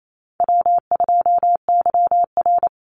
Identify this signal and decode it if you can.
Morse code, heard as a keyed tone, about 28 wpm, 700 Hz. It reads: W2YR
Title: Call Signs - 28wpm